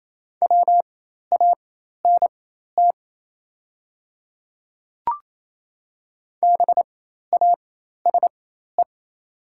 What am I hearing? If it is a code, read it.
Morse code, heard as a keyed tone, about 28 wpm, spaced as for 13 wpm, 700 Hz. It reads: WANT BASE
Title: 4x Character Spacing - Top 300 Words - 28wpm